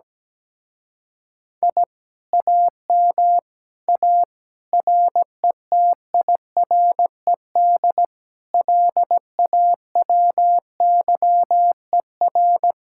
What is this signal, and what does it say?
Morse code, heard as a keyed tone, about 17 wpm, 700 Hz. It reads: I AM A RETIRED LAWYER
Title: Speed-Racing Common QSO Phrases - 17wpm